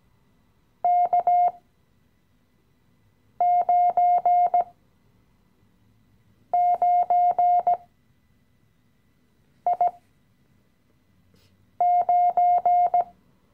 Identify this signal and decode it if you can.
Morse code, heard as a keyed tone, about 17 wpm, spaced as for 4 wpm, 710 Hz. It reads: K99I9
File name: CW-Online-Kurs / Anfänger / Lektion 015